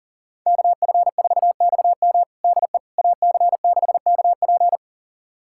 Morse code, heard as a keyed tone, about 40 wpm, 700 Hz. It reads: KF4XM DE AC6KP
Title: USA Callsign DE Callsign - 5 Characters - 40wpm